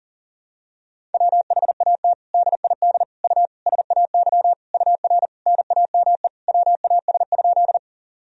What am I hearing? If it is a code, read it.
Morse code, heard as a keyed tone, about 40 wpm, 700 Hz. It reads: WHAT DID U SAY UR NAME WAS?
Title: Sentences from Top 200 Words and Abbreviations 40wpm